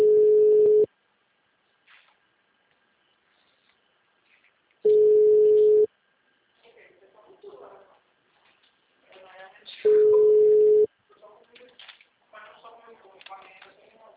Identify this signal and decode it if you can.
Ringback tone (Europe, 425 Hz), with speech.
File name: out- -2004- - - .14